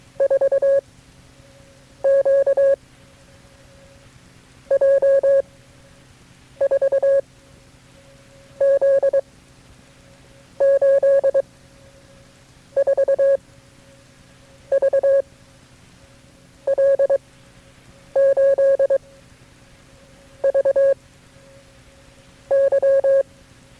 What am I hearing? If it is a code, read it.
Morse code, heard as a keyed tone, about 23 wpm, spaced as for 8 wpm, 550 Hz. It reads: 4 Q J 4 Z 8 4 V L8 V Y
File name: KB6MT-tape3-side1